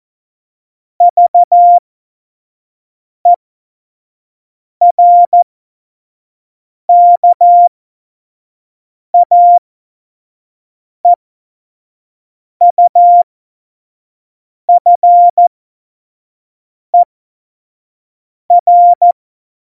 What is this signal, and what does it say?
Morse code, heard as a keyed tone, about 14 wpm, spaced as for 5 wpm, 700 Hz. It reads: VERKAEUFER
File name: Morsekurs / Witze / Witz 22. Januar 2019, 25 F70